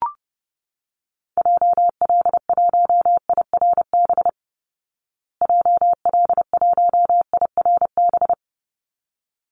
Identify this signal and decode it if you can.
Morse code, heard as a keyed tone, about 30 wpm, 700 Hz. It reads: JL1IRB JL1IRB
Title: Call Signs Repeated 3x - 30wpm